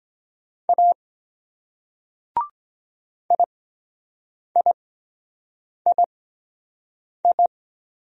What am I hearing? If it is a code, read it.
Morse code, heard as a keyed tone, about 21 wpm, spaced as for 7 wpm, 700 Hz. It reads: A IIII
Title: Speed-Racing Single Letters 17wpm